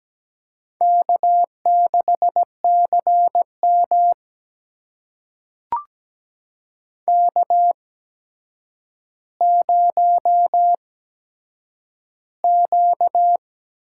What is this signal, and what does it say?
Morse code, heard as a keyed tone, about 17 wpm, spaced as for 8 wpm, 700 Hz. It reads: K6CM K0Q
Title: USA Call Signs - 8x Character Spacing - 17wpm